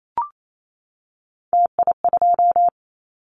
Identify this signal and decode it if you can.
Morse code, heard as a keyed tone, about 28 wpm, 700 Hz. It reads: TI2
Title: Callsign Prefixes for Top 100 DXCC - 28wpm